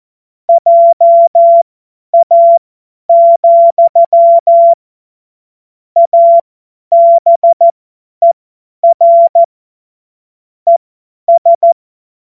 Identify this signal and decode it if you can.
Morse code, heard as a keyed tone, about 14 wpm, spaced as for 10 wpm, 680 Hz. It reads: JA, ABER ES
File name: Morsekurs / Witze / Witz 11. Mai 2021, 50 F70